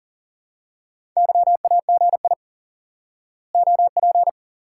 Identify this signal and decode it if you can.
Morse code, heard as a keyed tone, about 40 wpm, 700 Hz. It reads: YAGI OP
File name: IQE Course 3 - Lesson 89 - YAGI - Two Elements - 40wpm